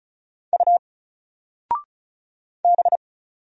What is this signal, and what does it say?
Morse code, heard as a keyed tone, about 34 wpm, 700 Hz. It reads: U B
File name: Single-Letter-Number-35wpm